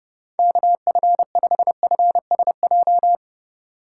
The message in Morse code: KF5FSJ

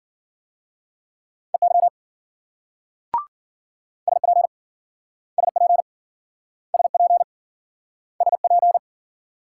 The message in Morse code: EY SP SP SP SP

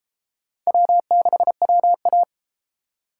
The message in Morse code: W6WA

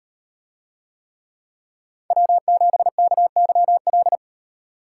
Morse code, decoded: W7KYL

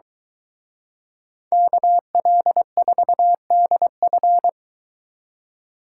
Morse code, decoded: KL4DF

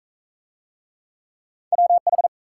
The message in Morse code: WH

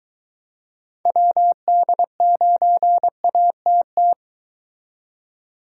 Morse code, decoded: WD9ATT

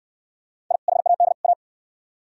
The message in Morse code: I HAD A

